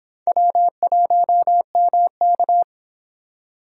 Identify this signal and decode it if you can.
Morse code, heard as a keyed tone, about 30 wpm, 700 Hz. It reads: W1MK